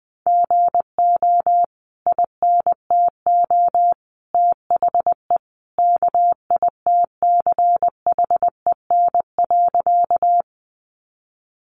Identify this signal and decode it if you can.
Morse code, heard as a keyed tone, about 20 wpm, 700 Hz. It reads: GO INTO THE KITCHEN.
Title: Sentences from Top 1400 Words 20wpm